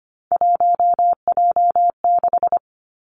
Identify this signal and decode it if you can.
Morse code, heard as a keyed tone, about 25 wpm, 700 Hz. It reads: 1J6
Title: Three Challenging Letter-Number Combinations 25wpm